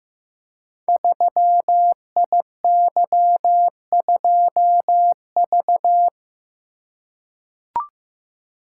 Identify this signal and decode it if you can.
Morse code, heard as a keyed tone, about 15 wpm, 700 Hz. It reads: 3IY2V
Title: Code Groups - 5 Letter:Numbers - 15wpm